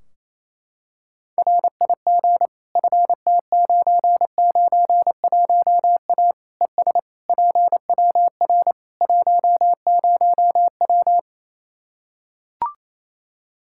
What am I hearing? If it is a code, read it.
Morse code, heard as a keyed tone, about 28 wpm, 700 Hz. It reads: RIG FT991A ES PWR 10W
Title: Practice QSO - Exchange 2 - Info - 28wpm